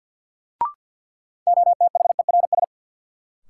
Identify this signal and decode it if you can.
Morse code, heard as a keyed tone, about 53 wpm, 700 Hz. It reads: OTHERS
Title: Sentences from Top 1500 Words - New Word Drill Alt - 50wpm